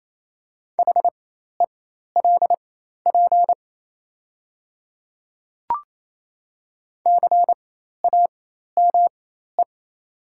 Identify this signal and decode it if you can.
Morse code, heard as a keyed tone, about 28 wpm, spaced as for 13 wpm, 690 Hz. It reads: HELP CAME
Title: 4x Character Spacing - Top 200 Words - 28wpm